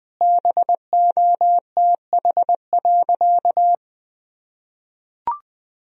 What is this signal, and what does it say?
Morse code, heard as a keyed tone, about 20 wpm, 700 Hz. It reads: BOTH.
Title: Sentences from Top 1000 Words - Review Part 1 - 20wpm